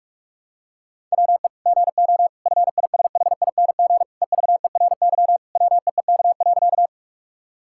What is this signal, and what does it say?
Morse code, heard as a keyed tone, about 46 wpm, 700 Hz. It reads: WE GO FISHING EVERY WEEK.
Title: Sentences from Top 1000 Words - Review Part 5 - 45wpm